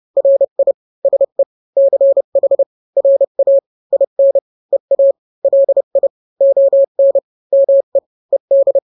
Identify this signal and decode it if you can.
Morse code, heard as a keyed tone, about 30 wpm, 550 Hz. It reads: RI SE CH RA IN EA LI ON ME ED
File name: n-grams / 2-grams 30 25 WPM